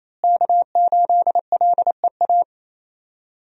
Morse code, heard as a keyed tone, about 28 wpm, 700 Hz. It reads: K8LEA
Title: USA Call Signs - 5 Characters - Repeated 3x - 28wpm